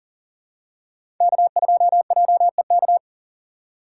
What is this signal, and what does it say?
Morse code, heard as a keyed tone, about 40 wpm, 700 Hz. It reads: K2JEK
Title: USA Call Signs - 3x Character Spacing - 40wpm